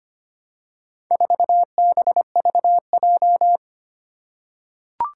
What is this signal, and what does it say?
Morse code, heard as a keyed tone, about 25 wpm, 700 Hz. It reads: 4BVJ
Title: Four Challenging Letter-Number Combinations 25wpm